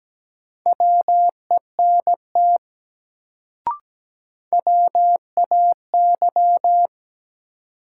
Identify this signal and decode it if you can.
Morse code, heard as a keyed tone, about 17 wpm, 700 Hz. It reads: WENT WAY
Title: Sets of 2 Words - New Word Drill - 17wpm